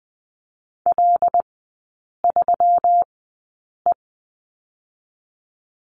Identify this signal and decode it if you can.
Morse code, heard as a keyed tone, about 20 wpm, spaced as for 8 wpm, 700 Hz. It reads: L3E
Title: 138 Lesson 23e - Letter f - Three Characters 20wpm